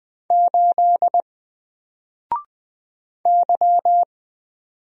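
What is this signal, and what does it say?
Morse code, heard as a keyed tone, about 20 wpm, 700 Hz. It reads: 8 Y
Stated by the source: Single letter and number Morse drill at 20 wpm, from N4LCD.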